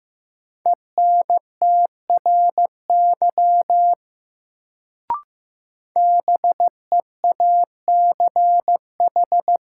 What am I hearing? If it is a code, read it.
Morse code, heard as a keyed tone, about 15 wpm, 700 Hz. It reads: ENTRY BEACH
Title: Sentences from Top 1500 Words - New Word Drill Alt - 15wpm